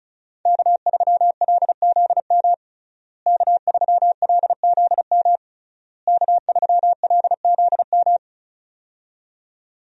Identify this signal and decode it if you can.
Morse code, heard as a keyed tone, about 36 wpm, 700 Hz. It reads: K3LZM K3LZM K3LZM
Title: USA Call Signs - 5 Characters - Repeated 3x - 35wpm